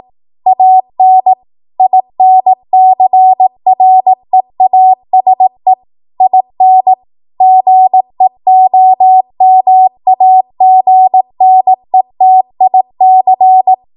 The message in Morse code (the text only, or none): AN INCREASE IN GEOMAGNETIC